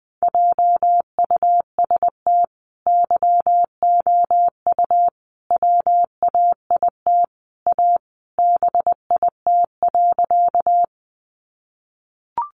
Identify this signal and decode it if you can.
Morse code, heard as a keyed tone, about 20 wpm, 700 Hz. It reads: JUST YOU WAIT A BIT.
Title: Sentences from Top 900 Words 20wpm